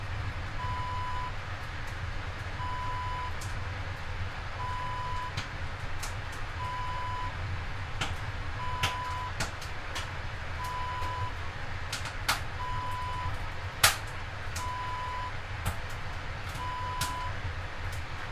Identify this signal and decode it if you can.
There is no code.